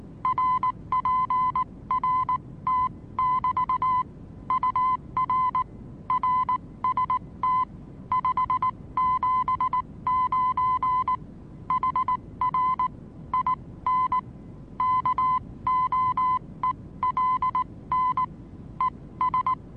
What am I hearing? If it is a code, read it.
Morse code, heard as a keyed tone, about 19 wpm, spaced as for 16 wpm, 1020 Hz. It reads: RPRT<BT> UR RST 579 HR IN KOELN ES
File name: repos / Morsekurs / DJ4UF Morsen26